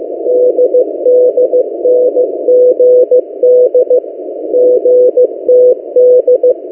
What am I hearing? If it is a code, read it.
Morse code, heard as a keyed tone, about 15 wpm, 500 Hz. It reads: DDNGD GTD
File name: M08a- -1355utc-8096khz